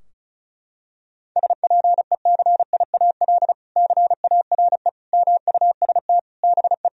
Morse code, heard as a keyed tone, about 34 wpm, 700 Hz. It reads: SPECIAL CARE MUST BE
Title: Sentences from Top 1000 Words - Review Part 4 - 35wpm